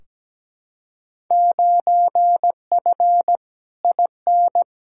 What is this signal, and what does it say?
Morse code, heard as a keyed tone, about 17 wpm, 700 Hz. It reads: 9F IN
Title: Field Day Exchanges Encore - 17wpm